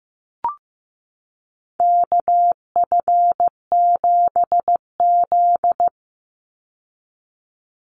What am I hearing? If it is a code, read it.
Morse code, heard as a keyed tone, about 15 wpm, 700 Hz. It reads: KF7Z
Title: Call Signs Repeated 3x - 15wpm